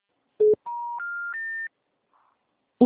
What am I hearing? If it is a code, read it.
Special information tone.